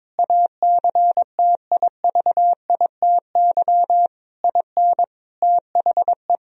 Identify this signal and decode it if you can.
Morse code, heard as a keyed tone, about 22 wpm, 700 Hz. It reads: ACTIVITY IN THE
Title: Sentences from Top 1000 Words - Review Part 4 - 22wpm